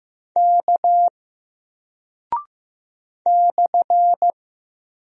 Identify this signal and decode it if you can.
Morse code, heard as a keyed tone, about 15 wpm, 700 Hz. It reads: K /
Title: Single Character (Letter:Number:Punctuation:Prosign) 15wpm